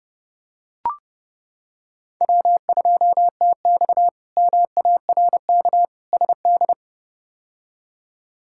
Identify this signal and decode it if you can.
Morse code, heard as a keyed tone, about 30 wpm, 700 Hz. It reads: W2TX MARK SD
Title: K1USN SST QSO Practice - 30wpm